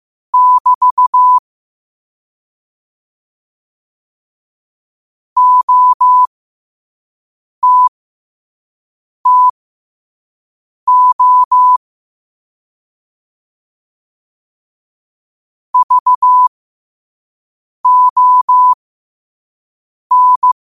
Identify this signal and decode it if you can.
Morse code, heard as a keyed tone, about 15 wpm, spaced as for 5 wpm, 1000 Hz. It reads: <BT> OTTO VON
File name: qotdfiles / 5wpm